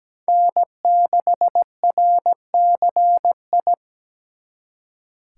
Morse code, heard as a keyed tone, about 20 wpm, 700 Hz. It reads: N6RCI
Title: Speed Racing - USA 1x3 Call Signs 17wpm